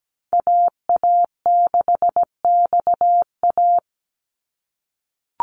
Morse code, heard as a keyed tone, about 17 wpm, 700 Hz. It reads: AA6XA
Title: NA SOTA Activator Callsigns 17wpm